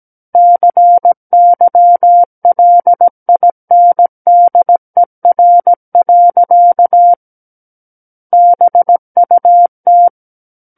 Morse code, heard as a keyed tone, about 17 wpm, 700 Hz. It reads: CYLINDER. BUT
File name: Book One / 08-Chapter 8 - 17 WPM